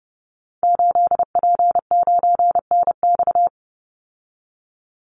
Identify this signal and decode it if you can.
Morse code, heard as a keyed tone, about 30 wpm, 700 Hz. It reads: 8P9NX
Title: Call Signs Repeated 2x - 30wpm